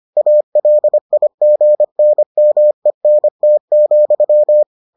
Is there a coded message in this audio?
Yes: Morse code from 0.2 to 4.6 s, heard as a keyed tone, about 25 wpm, 600 Hz. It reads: ALIGNMENT,